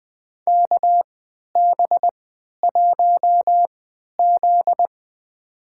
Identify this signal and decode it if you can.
Morse code, heard as a keyed tone, about 20 wpm, spaced as for 11 wpm, 700 Hz. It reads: KB1Z
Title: USA Call Signs - 3x Character Spacing - 20wpm